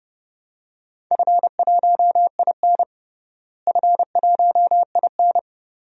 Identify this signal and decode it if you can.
Morse code, heard as a keyed tone, about 30 wpm, 700 Hz. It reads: F1IN F1IN
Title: DXCC-100 Call Signs - 4 Characters - Repeated 2x - 30wpm